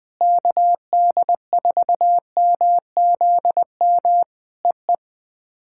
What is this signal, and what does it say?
Morse code, heard as a keyed tone, about 19 wpm, 700 Hz. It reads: KD4MZM EE
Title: POTA QSO Practice - 20wpm